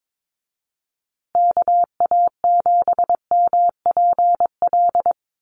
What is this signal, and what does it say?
Morse code, heard as a keyed tone, about 22 wpm, 700 Hz. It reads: KA7MPL
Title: USA Call Signs Repeated 2x 22wpm